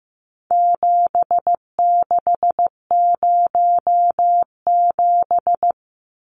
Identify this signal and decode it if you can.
Morse code, heard as a keyed tone, about 15 wpm, 700 Hz. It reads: 7607
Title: Numbers - 4-Digit 15wpm